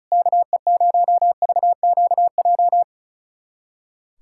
Morse code, heard as a keyed tone, about 34 wpm, 700 Hz. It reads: KE0VQJ
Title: USA Callsign DE Callsign 35wpm